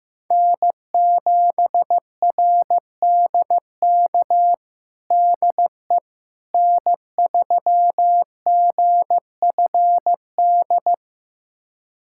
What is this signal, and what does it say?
Morse code, heard as a keyed tone, about 16 wpm, 700 Hz. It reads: N7RDK DE N3GFD
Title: USA Callsign DE Callsign - 5 Characters - 15wpm